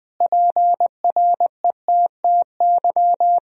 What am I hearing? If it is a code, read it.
Morse code, heard as a keyed tone, about 20 wpm, 700 Hz. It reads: PRETTY
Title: Sentences from Top 1000 Words - Review Part 2 - 20wpm